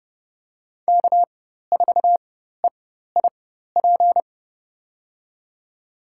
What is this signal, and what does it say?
Morse code, heard as a keyed tone, about 29 wpm, spaced as for 14 wpm, 700 Hz. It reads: K4EIP
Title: USA Call Signs - 5 Characters - 4x Character Spacing - 30wpm